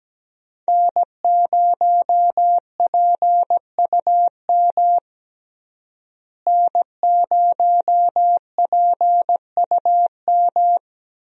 Morse code, heard as a keyed tone, about 17 wpm, 700 Hz. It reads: N0PUM N0PUM
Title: USA Call Signs - 5 Characters - Repeated 2x - 17wpm